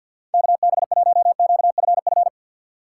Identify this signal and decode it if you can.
Morse code, heard as a keyed tone, about 49 wpm, 700 Hz. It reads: KB1QFF
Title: USA Call Signs - 6 Characters - 4x Character Spacing - 50wpm